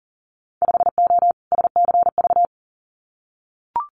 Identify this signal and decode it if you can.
Morse code, heard as a keyed tone, about 42 wpm, 700 Hz. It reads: 5O SCV
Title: Winter Field Day Exchanges Encore 40wpm